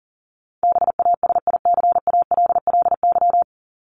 Morse code, heard as a keyed tone, about 40 wpm, 700 Hz. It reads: BASICALLY
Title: Top 1251-1500 Words 40wpm